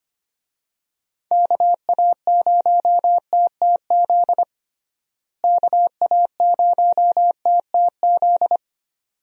Morse code, heard as a keyed tone, about 25 wpm, 700 Hz. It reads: KA0TTZ KA0TTZ
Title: USA Call Signs - 6 Characters - Repeated 2x - 25wpm